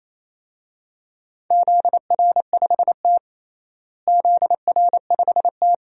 Morse code, heard as a keyed tone, about 27 wpm, 700 Hz. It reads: ZR5T ZR5T
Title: DXCC-100 Call Signs - 4 Characters - Repeated 2x - 28wpm